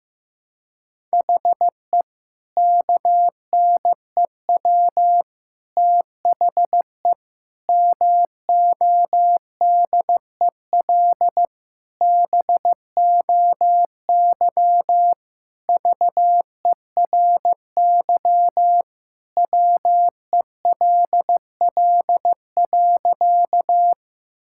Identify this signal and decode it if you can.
Morse code, heard as a keyed tone, about 15 wpm, 700 Hz. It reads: HE KNEW THE MODEL BOY VERY WELL.
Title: Sentences from Top 1000 Words - Review Part 5 - 15wpm